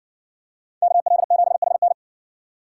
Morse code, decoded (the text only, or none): KL7HN